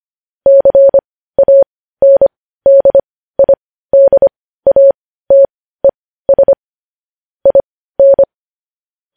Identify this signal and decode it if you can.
Morse code, heard as a keyed tone, about 25 wpm, spaced as for 15 wpm, 550 Hz. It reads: CANDIDATES IN
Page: News Headlines in Morse Code at 15 WPM